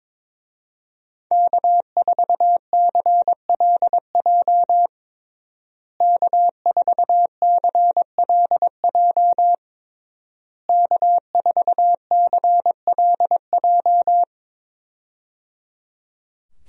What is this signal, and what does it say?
Morse code, heard as a keyed tone, about 22 wpm, 700 Hz. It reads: K4CLJ K4CLJ K4CLJ
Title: USA Call Signs - 5 Characters - Repeated 3x - 22wpm